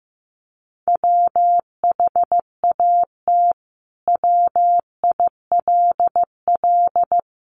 Morse code, heard as a keyed tone, about 15 wpm, 700 Hz. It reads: WHAT WILL
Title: Sentences from Top 600 Words 15wpm